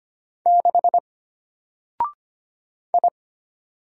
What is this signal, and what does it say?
Morse code, heard as a keyed tone, about 25 wpm, 700 Hz. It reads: 6 I
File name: Single-Letter-Number-25wpm